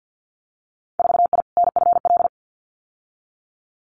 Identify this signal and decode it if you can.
Morse code, heard as a keyed tone, about 50 wpm, 700 Hz. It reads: 4I NFL
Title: Winter Field Day Exchanges Encore 50wpm